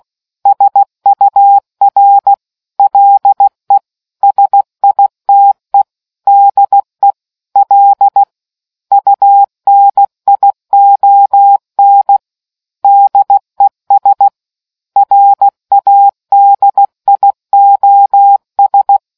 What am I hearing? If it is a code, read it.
Morse code, heard as a keyed tone, about 16 wpm, 800 Hz. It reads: SUR LE SITE DE L UNION DES RADIOS